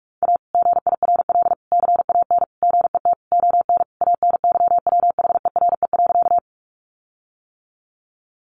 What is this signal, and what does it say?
Morse code, heard as a keyed tone, about 44 wpm, 700 Hz. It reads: A GIRL CAN GET ON ANYWHERE.
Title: Sentences from Top 1400 Words 45wpm